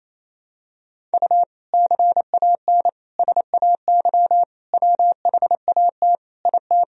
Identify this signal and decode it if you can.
Morse code, heard as a keyed tone, about 28 wpm, 700 Hz. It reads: U CAN SAY WHAT IT
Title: Sentences from Top 200 Words and Abbreviations 28wpm